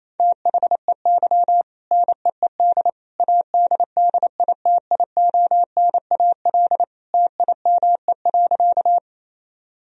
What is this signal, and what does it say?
Morse code, heard as a keyed tone, about 28 wpm, 700 Hz. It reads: THEY NEED ADDITIONAL TIME.
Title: Sentences from Top 1000 Words - Review Part 4 - 28wpm